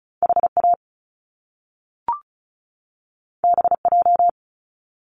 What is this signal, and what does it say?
Morse code, heard as a keyed tone, about 36 wpm, 700 Hz. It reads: HA BJ
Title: Top Two Letter Combinations in Words 35wpm